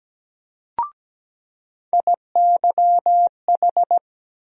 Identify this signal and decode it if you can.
Morse code, heard as a keyed tone, about 17 wpm, 700 Hz. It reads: IYH
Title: USA Call Sign Postfixes - 3 Letters 17wpm